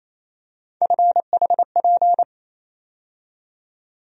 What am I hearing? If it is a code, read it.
Morse code, heard as a keyed tone, about 28 wpm, 700 Hz. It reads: FHP
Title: USA Call Sign Postfixes - 3 Letters 28wpm